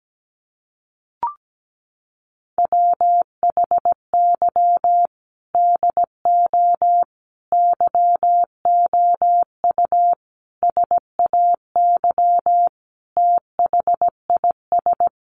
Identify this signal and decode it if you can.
Morse code, heard as a keyed tone, about 17 wpm, 700 Hz. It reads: WHY DO YOU SAY THIS
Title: Sentences from Top 1100 Words 17wpm